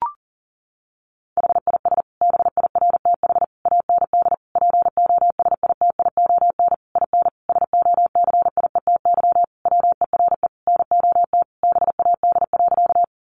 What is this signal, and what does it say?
Morse code, heard as a keyed tone, about 37 wpm, 700 Hz. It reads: HIS BIRTH AND POSITION IN SOCIETY WERE NOT BAD.
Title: Sentences from Top 900 Words 40wpm